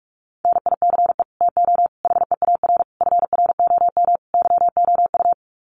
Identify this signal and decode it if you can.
Morse code, heard as a keyed tone, about 44 wpm, 700 Hz. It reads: NICE TO HEAR FROM YOU
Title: Common QSO Phrases 45wpm